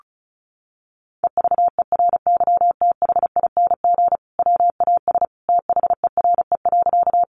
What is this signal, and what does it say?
Morse code, heard as a keyed tone, about 34 wpm, 700 Hz. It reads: EVERYTHING WAS THERE.
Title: Sentences from Top 700 Words 35wpm